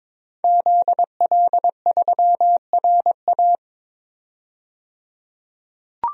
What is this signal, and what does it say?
Morse code, heard as a keyed tone, about 22 wpm, 700 Hz. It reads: ZL3RA
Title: DXCC-100 Call Signs - 5 Characters - 22wpm